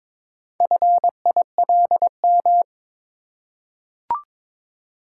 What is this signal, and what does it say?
Morse code, heard as a keyed tone, about 22 wpm, 700 Hz. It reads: FILM
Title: Four Letter Words - 22wpm